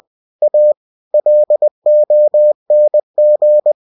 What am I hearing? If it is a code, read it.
Morse code, heard as a keyed tone, about 20 wpm, 600 Hz. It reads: A LONG
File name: cw-books / winnie-the-pooh / winnie-20wpm-page-0079